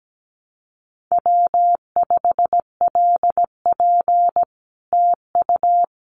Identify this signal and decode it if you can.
Morse code, heard as a keyed tone, about 17 wpm, 700 Hz. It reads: W5LP TU
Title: Common QSO Phrases Encore 17wpm